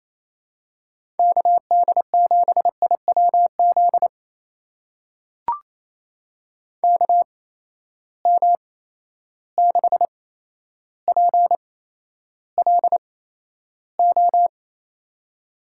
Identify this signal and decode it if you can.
Morse code, heard as a keyed tone, about 27 wpm, spaced as for 12 wpm, 700 Hz. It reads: KD7IWZ KM6PLO
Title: USA Call Signs - 6 Characters - 8x Character Spacing - 28wpm